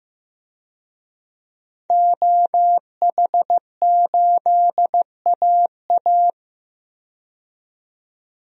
Morse code, heard as a keyed tone, about 15 wpm, 700 Hz. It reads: OH8AA
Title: DXCC-100 Call Signs - 5 Characters - 15wpm